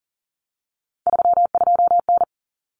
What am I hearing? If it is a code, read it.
Morse code, heard as a keyed tone, about 40 wpm, 700 Hz. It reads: 32N
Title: Common Cut Numbers - 3-Digit 40wpm